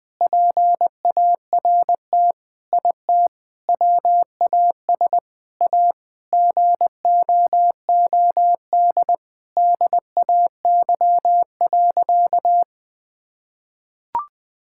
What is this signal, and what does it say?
Morse code, heard as a keyed tone, about 21 wpm, 700 Hz. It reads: PART IT WAS A GOOD DAY.
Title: Sentences with 4-Word Phrases - Spoken Beginning - 20wpm